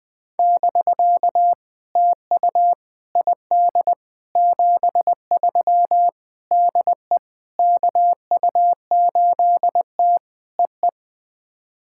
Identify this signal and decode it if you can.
Morse code, heard as a keyed tone, about 21 wpm, 700 Hz. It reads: <BK> TU ID 73 DE KU8T EE
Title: POTA QSO Practice - 20wpm